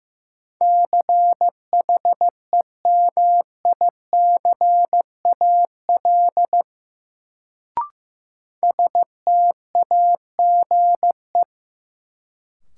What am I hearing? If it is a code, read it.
Morse code, heard as a keyed tone, about 15 wpm, 700 Hz. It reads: CHEMICAL STAGE
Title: Sentences from Top 1200 Words - New Word Drill Alt - 15wpm